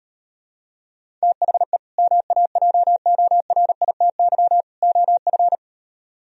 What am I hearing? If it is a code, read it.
Morse code, heard as a keyed tone, about 38 wpm, 700 Hz. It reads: THE MAJORITY OF